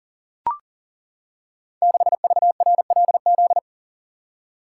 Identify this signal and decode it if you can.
Morse code, heard as a keyed tone, about 38 wpm, 700 Hz. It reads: 6VRLZ
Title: Code Groups - 5 Letter:Numbers - 40wpm